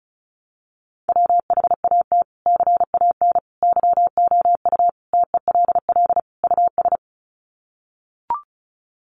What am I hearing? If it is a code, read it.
Morse code, heard as a keyed tone, about 34 wpm, 700 Hz. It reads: WHAT CAN YOU TELL US
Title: Sets of 5 Words Encore 35wpm